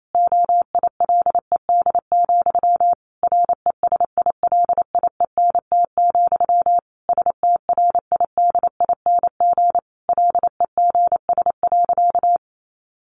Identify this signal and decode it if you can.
Morse code, heard as a keyed tone, about 28 wpm, 700 Hz. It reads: OILED, RESILIENT, STRIDING LEGS.
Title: A Sound Of Thunder 28wpm